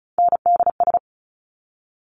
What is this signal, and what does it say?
Morse code, heard as a keyed tone, about 36 wpm, 700 Hz. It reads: NDS
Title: Top Three Letter Combinations in Words 35wpm